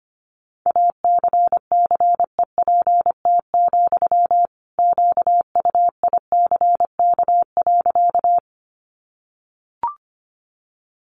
Morse code, heard as a keyed tone, about 25 wpm, 700 Hz. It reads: ACCEPT, QUICK.
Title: Sentences from Top 1300 Words 25wpm